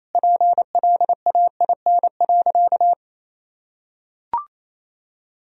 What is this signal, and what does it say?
Morse code, heard as a keyed tone, about 28 wpm, 700 Hz. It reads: PLAIN.
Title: Sentences from Top 1000 Words - Review Part 4 - 28wpm